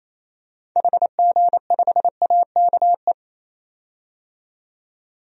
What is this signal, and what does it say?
Morse code, heard as a keyed tone, about 28 wpm, 700 Hz. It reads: HG5AKE
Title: DXCC-100 Call Signs - 6 Characters - 28wpm